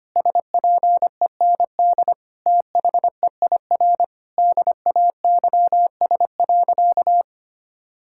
Morse code, heard as a keyed tone, about 25 wpm, 700 Hz. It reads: SPEND THEIR DAYS.
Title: Sentences from Top 1000 Words - Review Part 5 - 25wpm